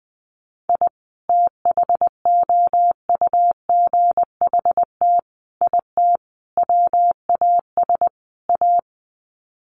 Morse code, heard as a keyed tone, about 20 wpm, 700 Hz. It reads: I THOUGHT IT WAS A
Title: Sets of 5 Words 20wpm